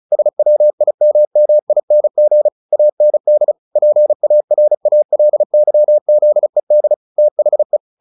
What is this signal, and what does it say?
Morse code, heard as a keyed tone, about 34 wpm, 600 Hz. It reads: SWIMMING AND PARALYZED THE
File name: the old man and the sea 2 35wpm